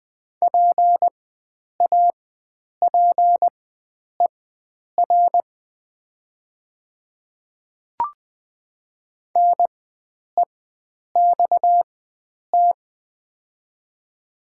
Morse code, heard as a keyed tone, about 20 wpm, spaced as for 9 wpm, 700 Hz. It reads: PAPER NEXT
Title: 4x Character Spacing - Top 300 Words - 20wpm